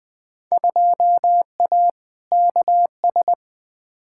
Morse code, heard as a keyed tone, about 20 wpm, 700 Hz. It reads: 2A KS
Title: Field Day Exchanges - 20wpm